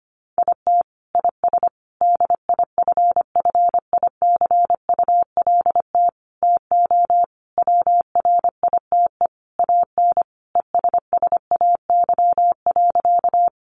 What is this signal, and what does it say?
Morse code, heard as a keyed tone, about 25 wpm, 700 Hz. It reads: IT IS DIFFICULT TO WRITE AN ESSAY.
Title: Sentences from Top 1500 Words 25wpm